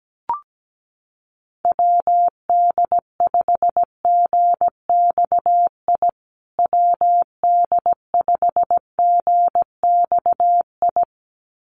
Morse code, heard as a keyed tone, about 17 wpm, 700 Hz. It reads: WD5GXI WD5GXI
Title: USA Call Signs Repeated 2x 17wpm